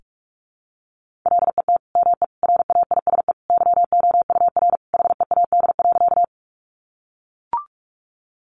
Morse code, heard as a keyed tone, about 44 wpm, 700 Hz. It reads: LET ME RAISE YOUR HEAD.
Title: Sentences from Top 1400 Words 45wpm